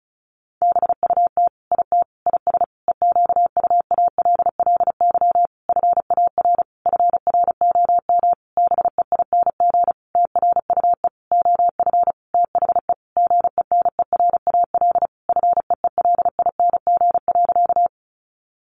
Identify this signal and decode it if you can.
Morse code, heard as a keyed tone, about 34 wpm, 700 Hz. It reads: BUT IT IS EQUALLY FAR FROM BEING TRUE OF THE GENERAL FEELING.
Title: Sentences from Top 1400 Words 35wpm